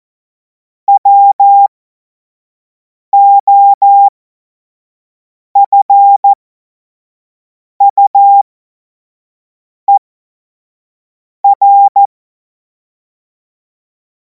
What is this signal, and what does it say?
Morse code, heard as a keyed tone, about 14 wpm, spaced as for 5 wpm, 800 Hz. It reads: WOFUER
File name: Morsekurs / Witze / Witz 02. Juni 2026 25 F70